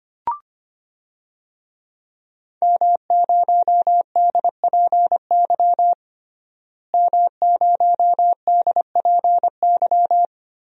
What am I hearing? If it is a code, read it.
Morse code, heard as a keyed tone, about 25 wpm, 700 Hz. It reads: M0DPY M0DPY
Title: DXCC-100 Call Signs - 5 Characters - Repeated 3x - 25wpm